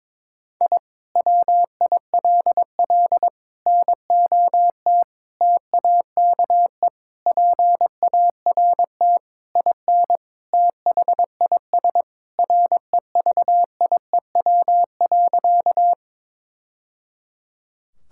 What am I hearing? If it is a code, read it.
Morse code, heard as a keyed tone, about 22 wpm, 700 Hz. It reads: I WILL NOT TAKE PART IN THIS REVIEW.
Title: Sentences from Top 1000 Words - Review Part 6 - 22wpm